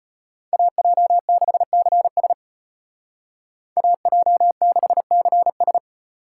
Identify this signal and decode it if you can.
Morse code, heard as a keyed tone, about 33 wpm, 700 Hz. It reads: AJ6CS AJ6CS